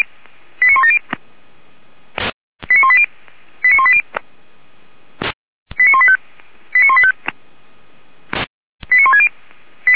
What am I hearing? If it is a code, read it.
Five-tone selective call (ZVEI), heard as quick runs of five tones, tones of about 70 ms. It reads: 89170 89180 89180 89185 89185 89160